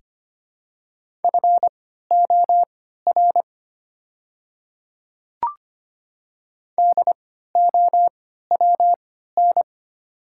Morse code, heard as a keyed tone, about 25 wpm, spaced as for 14 wpm, 700 Hz. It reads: FOR DOWN
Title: 3x Character Spacing - Top 100 Words - 25wpm